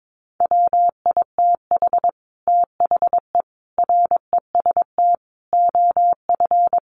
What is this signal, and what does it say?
Morse code, heard as a keyed tone, about 22 wpm, 700 Hz. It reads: WITH THE REST OF
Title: Sets of 5 Words 22wpm